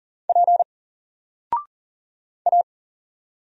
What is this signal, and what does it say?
Morse code, heard as a keyed tone, about 40 wpm, 700 Hz. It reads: P A
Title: Single Character (Letter:Number:Punctuation:Prosign) 40wpm